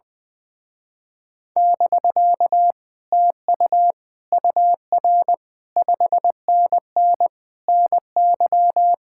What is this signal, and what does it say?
Morse code, heard as a keyed tone, about 20 wpm, 700 Hz. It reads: <BK> TU UR 5NN NY
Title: Speed-Racing Common QSO Phrases - 20wpm